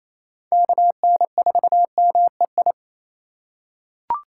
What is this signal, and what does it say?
Morse code, heard as a keyed tone, about 28 wpm, 700 Hz. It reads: KN4MEI
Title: USA Call Signs - 6 Characters - 4x Character Spacing - 28wpm